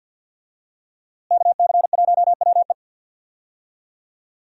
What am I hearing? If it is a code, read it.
Morse code, heard as a keyed tone, about 49 wpm, 700 Hz. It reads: KC1WE
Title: USA Call Signs - 5 Characters - Repeated 2x - 50wpm